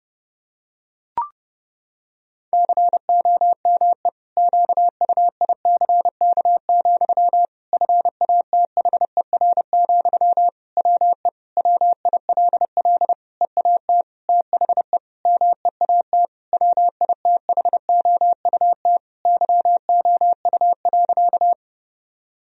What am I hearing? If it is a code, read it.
Morse code, heard as a keyed tone, about 32 wpm, 700 Hz. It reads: COME QUICK, FATHER, WE WILL EAT THE MEAT WITHOUT YOU.
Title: Sentences from Top 1000 Words - Review Part 3 - 30wpm